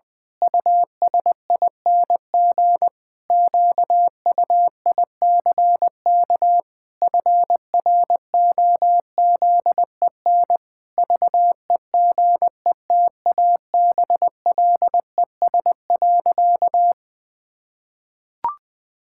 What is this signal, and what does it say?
Morse code, heard as a keyed tone, about 20 wpm, 700 Hz. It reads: USING QUICK FROZEN VEGETABLES.
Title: Pangram Challenge v2 - 20wpm